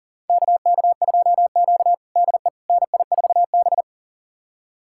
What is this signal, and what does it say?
Morse code, heard as a keyed tone, about 37 wpm, 700 Hz. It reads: KK2Q DE NI4B
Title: USA Callsign DE Callsign - 4 Characters - 40wpm